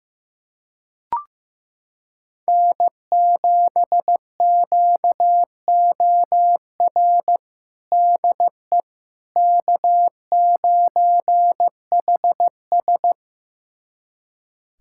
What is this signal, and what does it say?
Morse code, heard as a keyed tone, about 15 wpm, 700 Hz. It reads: N7QOR DE K9HS
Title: USA Callsign DE Callsign 15wpm